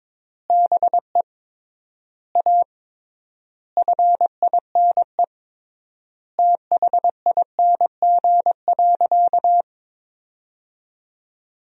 Morse code, heard as a keyed tone, about 22 wpm, 700 Hz. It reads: BE A FINE THING.
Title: 3x Word-Spacing - Sentences from Top 500 Words - 22wpm